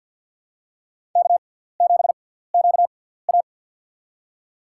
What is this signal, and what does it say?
Morse code, heard as a keyed tone, about 54 wpm, spaced as for 17 wpm, 700 Hz. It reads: K7QA
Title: USA Call Signs - 4 Characters - 6x Character Spacing - 50wpm